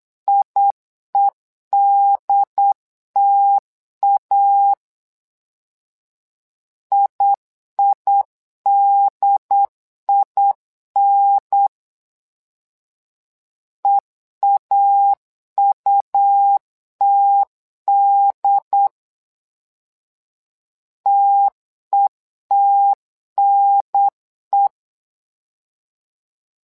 Morse code, heard as a keyed tone, about 8 wpm, 800 Hz. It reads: IEDTA IIDIN EAUTD TETNE
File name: LEZIONI / 2 UHDN / 2 UHDN 8wpm2232